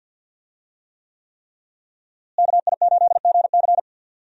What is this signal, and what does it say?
Morse code, heard as a keyed tone, about 52 wpm, 700 Hz. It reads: KI8GC
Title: USA Call Signs - 5 Characters - Repeated 3x - 50wpm